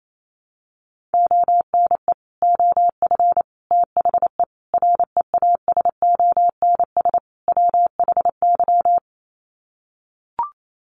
Morse code, heard as a keyed tone, about 28 wpm, 700 Hz. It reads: ONE OF THE REASONS WHY
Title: Sets of 5 Words 28wpm